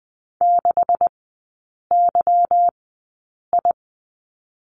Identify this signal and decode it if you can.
Morse code, heard as a keyed tone, about 20 wpm, spaced as for 8 wpm, 700 Hz. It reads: 6YI
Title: 184 Lesson 30e - Letter b - Three Characters 20wpm